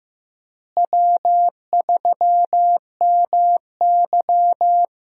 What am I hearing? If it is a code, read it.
Morse code, heard as a keyed tone, about 15 wpm, 700 Hz. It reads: W3MY
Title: USA Call Signs - 4 Characters - Repeated 3x - 15wpm